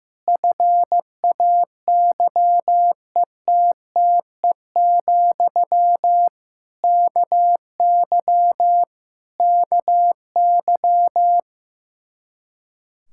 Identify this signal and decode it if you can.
Morse code, heard as a keyed tone, about 15 wpm, 700 Hz. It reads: FAYETTE, KY KY
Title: Practice QSO - Exchange 1 - Beginning - 15wpm